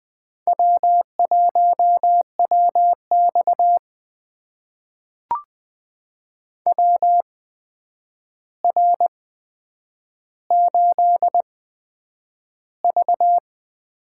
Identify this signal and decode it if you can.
Morse code, heard as a keyed tone, about 20 wpm, spaced as for 9 wpm, 700 Hz. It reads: W1WX WR8V
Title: USA Call Signs - 4 Characters - 8x Character Spacing - 20wpm